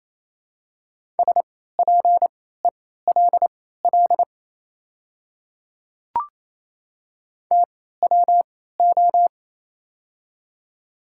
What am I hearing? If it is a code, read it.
Morse code, heard as a keyed tone, about 27 wpm, spaced as for 16 wpm, 700 Hz. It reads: SPELL TWO
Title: 3x Character Spacing - Top 200 Words - 28wpm